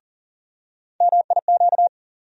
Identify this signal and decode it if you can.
Morse code, heard as a keyed tone, about 40 wpm, 700 Hz. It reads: MIQ
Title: USA Call Sign Postfixes - 3 Letters 40wpm